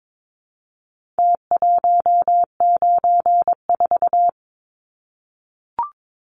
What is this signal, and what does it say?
Morse code, heard as a keyed tone, about 22 wpm, 700 Hz. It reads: T194
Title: Numbers - Contest Style - 4-Digit 22wpm